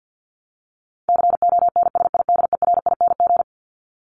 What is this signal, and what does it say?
Morse code, heard as a keyed tone, about 50 wpm, 700 Hz. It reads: CONSIDERING
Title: ING Suffix - Top 100 Words with Base Words 50wpm